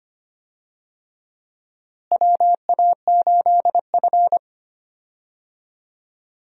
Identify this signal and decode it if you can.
Morse code, heard as a keyed tone, about 25 wpm, 700 Hz. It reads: WA8F
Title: USA Call Signs - 4 Characters - Repeated 3x - 25wpm